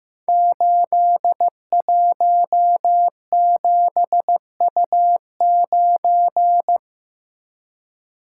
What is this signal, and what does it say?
Morse code, heard as a keyed tone, about 15 wpm, 700 Hz. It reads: 817U9
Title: Code Groups - 5 Letter:Numbers - 15wpm